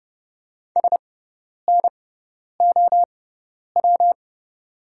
Morse code, heard as a keyed tone, about 31 wpm, spaced as for 10 wpm, 700 Hz. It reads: SNOW
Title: 6x Character Spacing - Top 500 Words - 30wpm